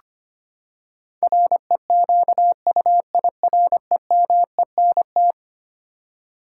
Morse code, heard as a keyed tone, about 25 wpm, 700 Hz. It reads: REQUIREMENT
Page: morse-code-ninja-practice-25wpm